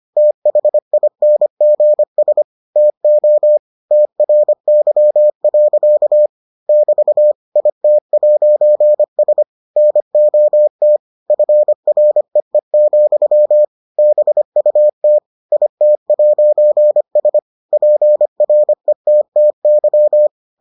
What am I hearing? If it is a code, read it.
Morse code, heard as a keyed tone, about 25 wpm, 600 Hz. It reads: THINGS TO TRY. <BT> IT'S NOT FREE, BUT IT'S PRETTY